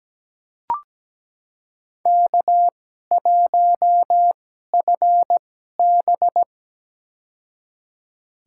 Morse code, heard as a keyed tone, about 17 wpm, spaced as for 12 wpm, 700 Hz. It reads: K1FB
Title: USA Call Signs - 4 Characters - 2x Character Spacing - 17wpm